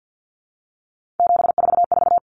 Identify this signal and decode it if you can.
Morse code, heard as a keyed tone, about 50 wpm, 700 Hz. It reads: 744
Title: Numbers - 3-Digit 50wpm